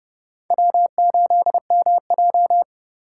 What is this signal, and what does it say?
Morse code, heard as a keyed tone, about 30 wpm, 700 Hz. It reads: W8MJ
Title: USA Call Signs - 4 Characters - 4x Character Spacing - 30wpm